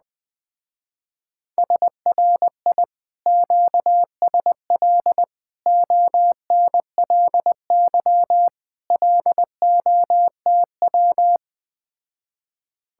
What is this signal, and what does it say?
Morse code, heard as a keyed tone, about 20 wpm, 700 Hz. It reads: SRI QSL ONLY LOTW
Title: Speed-Racing Common QSO Phrases Encore - 17wpm